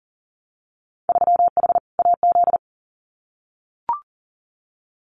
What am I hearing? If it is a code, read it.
Morse code, heard as a keyed tone, about 41 wpm, 700 Hz. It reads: 3H AZ
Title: Winter Field Day Exchanges Encore 40wpm